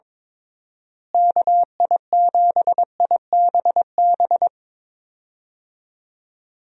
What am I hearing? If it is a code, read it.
Morse code, heard as a keyed tone, about 22 wpm, 700 Hz. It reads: KI7IBB
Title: USA Call Signs - 6 Characters - Repeated 2x - 22wpm